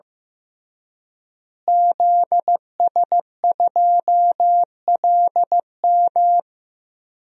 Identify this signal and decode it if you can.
Morse code, heard as a keyed tone, about 15 wpm, 700 Hz. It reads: ZS2LM
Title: DXCC-100 Call Signs - 5 Characters - Repeated 3x - 15wpm